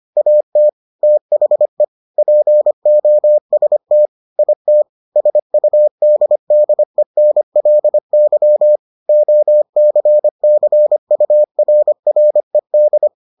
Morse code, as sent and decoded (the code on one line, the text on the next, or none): .- - / - .... / .--. --- ... - / .. - / ... ..- -.. -.. -. .-.. -.-- / --- -.-. -.-. ..- .-. .-. -..
AT THE POST IT SUDDENLY OCCURRED